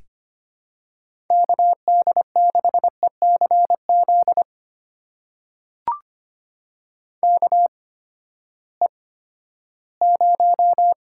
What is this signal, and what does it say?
Morse code, heard as a keyed tone, about 25 wpm, spaced as for 14 wpm, 700 Hz. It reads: KD6ECZ KE0
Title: USA Call Signs - 6 Characters - 8x Character Spacing - 25wpm